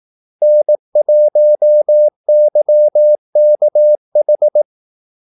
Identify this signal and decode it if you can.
Morse code, heard as a keyed tone, about 18 wpm, 600 Hz. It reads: N1YKH
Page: Kaldesignaler 18 wpm